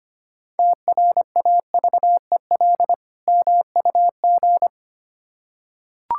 Morse code, heard as a keyed tone, about 25 wpm, 700 Hz. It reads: TRAVEL MUG
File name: practice / 25wpm / Difficult Words with Hints - 25wpm